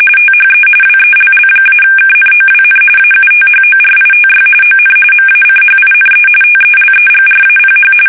RTTY, about 45.45 baud. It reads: THE WELL-KNOWN AUTHORITY ON MACHINE TELEGRAPH